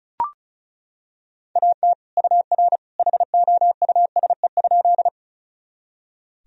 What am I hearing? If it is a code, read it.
Morse code, heard as a keyed tone, about 36 wpm, 700 Hz. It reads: AT UR HOUSE?
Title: Sentences from Top 200 Words and Abbreviations 35wpm